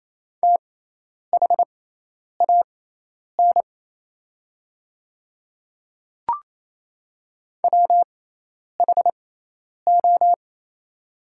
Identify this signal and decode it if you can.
Morse code, heard as a keyed tone, about 28 wpm, spaced as for 10 wpm, 700 Hz. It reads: THAN WHO
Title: 6x Character Spacing - Top 100 Words - 28wpm